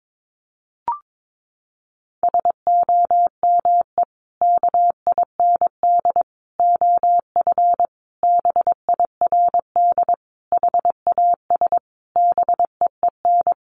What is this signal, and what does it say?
Morse code, heard as a keyed tone, about 22 wpm, 700 Hz. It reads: SOME KIND OF BIRD HAS BEEN
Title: Sentences from Top 1200 Words 22wpm